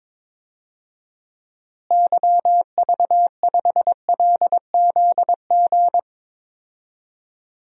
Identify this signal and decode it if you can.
Morse code, heard as a keyed tone, about 21 wpm, 700 Hz. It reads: YV5LZG
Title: DXCC-100 Call Signs - 6 Characters - Repeated 3x - 22wpm